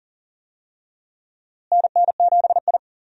Morse code, heard as a keyed tone, about 40 wpm, 700 Hz. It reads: NN7I
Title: USA Call Signs - 4 Characters - Repeated 2x - 40wpm